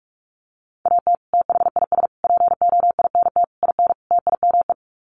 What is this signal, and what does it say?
Morse code, heard as a keyed tone, about 44 wpm, 700 Hz. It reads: AT THIS POINT IN TIME
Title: Sets of 5 Words Encore 45wpm